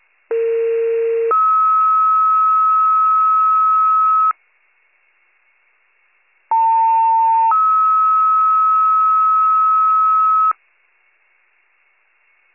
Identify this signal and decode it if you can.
Two-tone page, heard as a one-second tone, then a three-second tone, about 460 then 1290 Hz, about 900 then 1290 Hz. It is heard from 0.3 to 10.5 s.